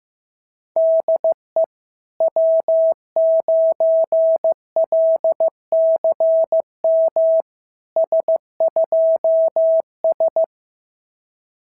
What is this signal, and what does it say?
Morse code, heard as a keyed tone, about 15 wpm, 650 Hz. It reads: DE W9LCM S2S
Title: SOTA QSO Practice 15wpm